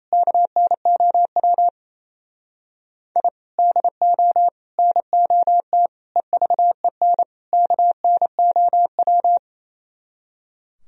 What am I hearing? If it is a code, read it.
Morse code, heard as a keyed tone, about 28 wpm, 700 Hz. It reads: KNOW I DO NOT EVEN KNOW
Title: Speed-Racing - Sets of 5 Words 28wpm